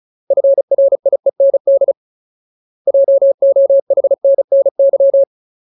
Morse code, heard as a keyed tone, about 34 wpm, 550 Hz. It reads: FRIEND JOHNNY
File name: Ultimate Word List / 35WPM Ultimate